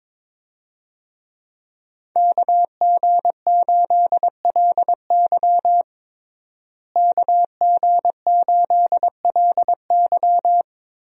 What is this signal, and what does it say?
Morse code, heard as a keyed tone, about 22 wpm, 700 Hz. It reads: KG8LY KG8LY
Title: USA Call Signs - 5 Characters - Repeated 3x - 22wpm